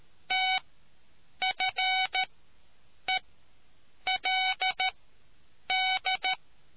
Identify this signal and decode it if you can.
Morse code, heard as a keyed tone, about 13 wpm, spaced as for 7 wpm, 750 Hz. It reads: TFELD